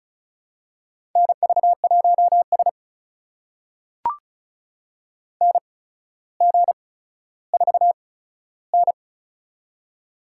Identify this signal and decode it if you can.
Morse code, heard as a keyed tone, about 36 wpm, spaced as for 15 wpm, 700 Hz. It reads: NV1S NG4N
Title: USA Call Signs - 4 Characters - 8x Character Spacing - 35wpm